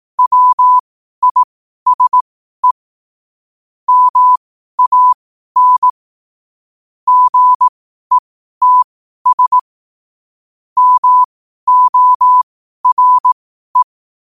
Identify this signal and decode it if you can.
Morse code, heard as a keyed tone, about 18 wpm, spaced as for 12 wpm, 1000 Hz. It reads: WISE MAN GETS MORE
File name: qotdfiles / 12wpm